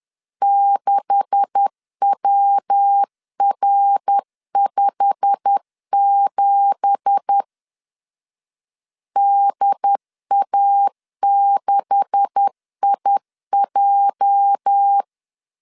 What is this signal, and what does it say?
Morse code, heard as a keyed tone, about 10 wpm, 800 Hz. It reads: 6WR57 DA6IJ
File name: CW / DET / 1 9 1500C S B 10wpm 4439